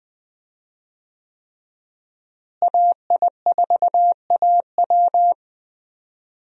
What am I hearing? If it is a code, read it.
Morse code, heard as a keyed tone, about 20 wpm, 700 Hz. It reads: AI4AW
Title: USA Call Signs - 5 Characters - Repeated 3x - 20wpm